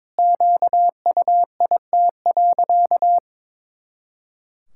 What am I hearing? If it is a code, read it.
Morse code, heard as a keyed tone, about 22 wpm, 700 Hz. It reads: QUIT.
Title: Pangram Challenge v2 - 22wpm